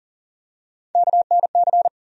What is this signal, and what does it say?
Morse code, heard as a keyed tone, about 40 wpm, 700 Hz. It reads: KNC
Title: USA Call Sign Postfixes - 3 Letters 40wpm